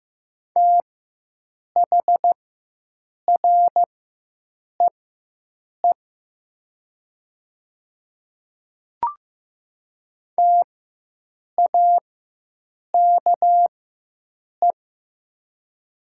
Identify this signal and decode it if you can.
Morse code, heard as a keyed tone, about 15 wpm, spaced as for 7 wpm, 700 Hz. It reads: THREE TAKE
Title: 4x Character Spacing - Top 200 Words - 15wpm